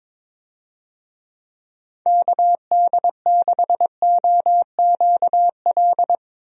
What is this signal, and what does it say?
Morse code, heard as a keyed tone, about 22 wpm, 700 Hz. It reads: KD6OQL
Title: USA Call Signs - 6 Characters - 22wpm